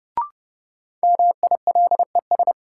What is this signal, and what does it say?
Morse code, heard as a keyed tone, about 30 wpm, 700 Hz. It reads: MILES
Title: Sentences from Top 1000 Words - Review Part 3 - New Word Drill - 30wpm